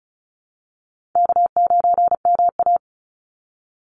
Morse code, heard as a keyed tone, about 34 wpm, 700 Hz. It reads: K9MA
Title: Call Signs Repeated 2x - 35wpm